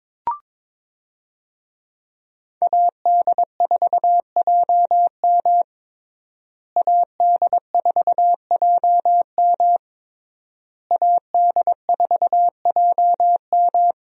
Morse code, heard as a keyed tone, about 22 wpm, 700 Hz. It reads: AD4JM AD4JM AD4JM
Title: USA Call Signs - 5 Characters - Repeated 3x - 22wpm